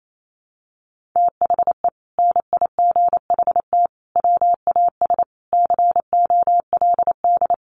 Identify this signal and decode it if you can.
Morse code, heard as a keyed tone, about 28 wpm, 700 Hz. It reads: THE NIGHT WAS COLD
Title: Sentences from Top 400 Words 28wpm